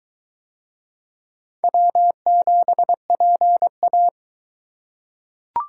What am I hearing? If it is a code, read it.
Morse code, heard as a keyed tone, about 20 wpm, 700 Hz. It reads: W7PA